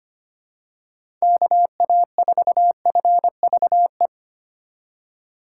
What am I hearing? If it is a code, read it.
Morse code, heard as a keyed tone, about 25 wpm, 700 Hz. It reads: KA4FVE
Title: USA Call Signs - 6 Characters - 3x Character Spacing - 25wpm